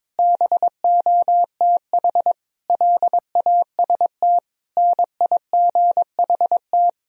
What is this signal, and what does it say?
Morse code, heard as a keyed tone, about 22 wpm, 700 Hz. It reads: BOTH LAST NIGHT
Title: Sentences from Top 1000 Words - Review Part 4 - 22wpm